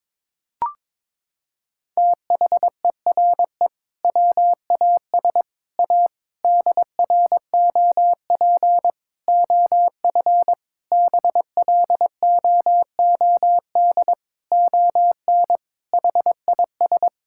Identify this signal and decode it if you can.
Morse code, heard as a keyed tone, about 22 wpm, 700 Hz. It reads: THERE WAS A DROP OF BLOOD ON HIS
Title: Sentences from Top 1600 Words 22wpm